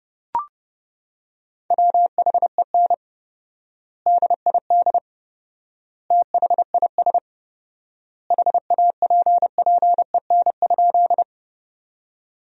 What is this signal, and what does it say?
Morse code, heard as a keyed tone, about 30 wpm, 700 Hz. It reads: WHEN DID THIS HAPPEN?
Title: 4x Word-Spacing - Sentences from Top 400 Words - 30wpm